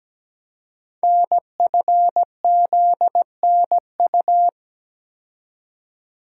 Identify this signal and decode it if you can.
Morse code, heard as a keyed tone, about 17 wpm, 700 Hz. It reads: NFZNU
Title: Code Groups - 5 Letters - 17wpm